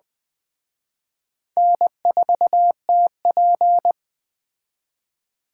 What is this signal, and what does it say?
Morse code, heard as a keyed tone, about 20 wpm, 700 Hz. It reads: N4TP
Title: Speed Racing - USA 1x2 Call Signs 17wpm